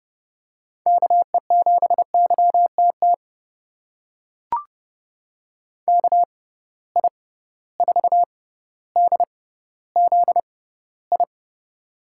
Morse code, heard as a keyed tone, about 32 wpm, spaced as for 16 wpm, 700 Hz. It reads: KE7YTT KI4DZI
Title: USA Call Signs - 6 Characters - 6x Character Spacing - 30wpm